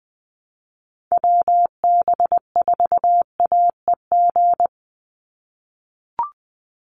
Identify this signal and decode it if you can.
Morse code, heard as a keyed tone, about 19 wpm, 700 Hz. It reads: WB4AEG
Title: Call Signs - 20wpm